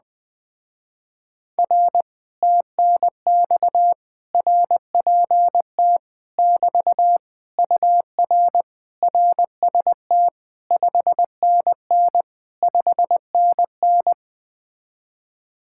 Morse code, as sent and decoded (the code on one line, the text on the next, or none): .-. / - -. -..- / .-. .--. - / -...- / ..- .-. / .-. ... - / ..... -. -. / ..... -. -.
R TNX RPT <BT> UR RST 5NN 5NN